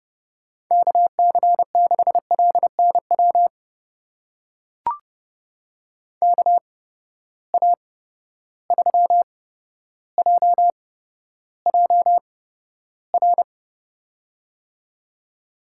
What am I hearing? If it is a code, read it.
Morse code, heard as a keyed tone, about 30 wpm, spaced as for 13 wpm, 700 Hz. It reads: KC6LNW KA3JJR
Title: USA Call Signs - 6 Characters - 8x Character Spacing - 30wpm